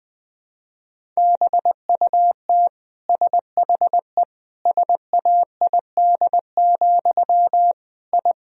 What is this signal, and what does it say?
Morse code, heard as a keyed tone, about 21 wpm, 700 Hz. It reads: BUT SHE SAID, I
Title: Sentences from Top 1000 Words - Review Part 1 - 20wpm